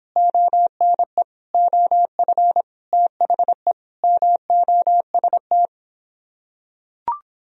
Morse code, heard as a keyed tone, about 26 wpm, 700 Hz. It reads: ONE OF THE MOST